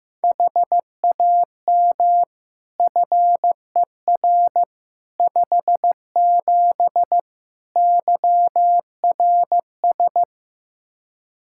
Morse code, heard as a keyed tone, about 15 wpm, 700 Hz. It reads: HAM FER 57 YRS
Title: K1USN SST QSO Practice 15wpm